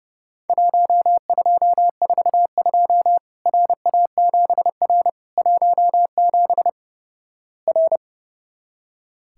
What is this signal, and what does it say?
Morse code, heard as a keyed tone, about 30 wpm, 700 Hz. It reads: 1242 RA7R 17 R
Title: Worked All Europe DX Contest - QTC Exchange - 30wpm